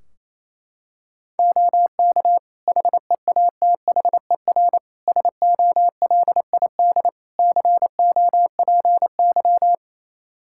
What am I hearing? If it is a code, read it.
Morse code, heard as a keyed tone, about 27 wpm, 700 Hz. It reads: OK HEATHER SOLID COPY
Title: K1USN SST QSO Practice 28wpm